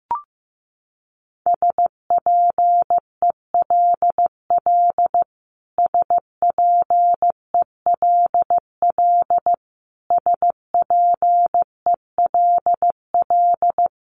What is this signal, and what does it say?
Morse code, heard as a keyed tone, about 15 wpm, 700 Hz. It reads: SPELL SPELL SPELL
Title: Top 500 Words Repeated 15wpm